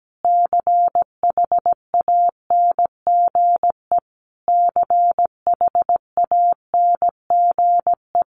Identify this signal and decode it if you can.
Morse code, heard as a keyed tone, about 17 wpm, 700 Hz. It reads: CHANGE CHANGE
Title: Top 200 Words Repeated 17wpm